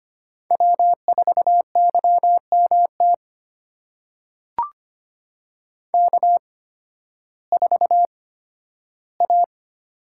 Morse code, heard as a keyed tone, about 25 wpm, spaced as for 13 wpm, 700 Hz. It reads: W4YMT K4A
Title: USA Call Signs - 5 Characters - 8x Character Spacing - 25wpm